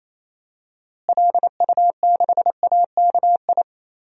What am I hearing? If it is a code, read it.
Morse code, heard as a keyed tone, about 28 wpm, 700 Hz. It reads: LU6AKI
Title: DXCC-100 Call Signs - 6 Characters - Repeated 3x - 28wpm